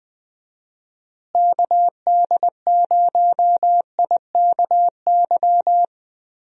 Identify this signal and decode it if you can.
Morse code, heard as a keyed tone, about 20 wpm, 700 Hz. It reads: KD0IKY
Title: USA Call Signs - 6 Characters - Repeated 3x - 20wpm